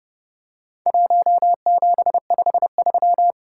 Morse code, heard as a keyed tone, about 30 wpm, 700 Hz. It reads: 1753
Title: Worked All Europe DX Contest - QTC Exchange - 30wpm